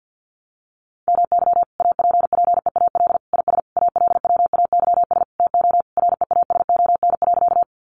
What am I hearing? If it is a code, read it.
Morse code, heard as a keyed tone, about 50 wpm, 700 Hz. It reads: MY APPEAL IS ALWAYS TO REASON.
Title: Sentences from Top 1300 Words 50wpm